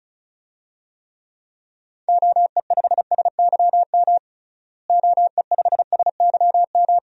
Morse code, heard as a keyed tone, about 36 wpm, 700 Hz. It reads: OE5SYM OE5SYM
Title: DXCC-100 Call Signs - 6 Characters - Repeated 2x - 35wpm